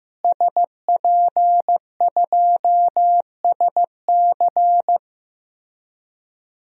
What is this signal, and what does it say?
Morse code, heard as a keyed tone, about 16 wpm, 700 Hz. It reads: SP2SC
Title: DXCC-100 Call Signs - 5 Characters - Repeated 3x - 15wpm